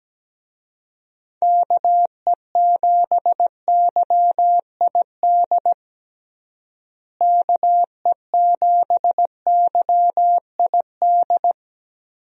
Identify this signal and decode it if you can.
Morse code, heard as a keyed tone, about 17 wpm, 700 Hz. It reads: KE7YID KE7YID
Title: USA Call Signs - 6 Characters - Repeated 3x - 17wpm